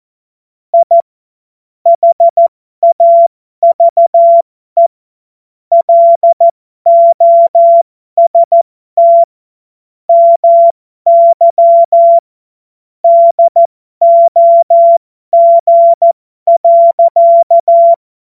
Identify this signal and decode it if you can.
Morse code, heard as a keyed tone, about 14 wpm, spaced as for 12 wpm, 680 Hz. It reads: I HAVE LOST MY DOG.
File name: Morsekurs / Witze